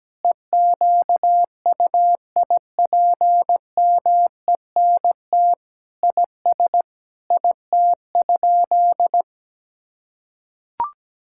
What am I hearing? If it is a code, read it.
Morse code, heard as a keyed tone, about 17 wpm, 700 Hz. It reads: EQUIPMENT IS IT?
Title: Sentences from Top 1000 Words - Review Part 5 - 17wpm